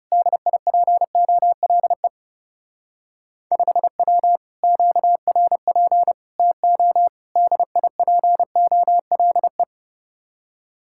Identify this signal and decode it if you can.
Morse code, heard as a keyed tone, about 38 wpm, 700 Hz. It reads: DIPOLE 5W QRP TO DIPOLE